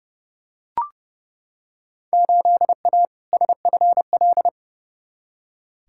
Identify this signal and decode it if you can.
Morse code, heard as a keyed tone, about 31 wpm, 700 Hz. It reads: 8A SFL
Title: Field Day Exchanges Encore - 30wpm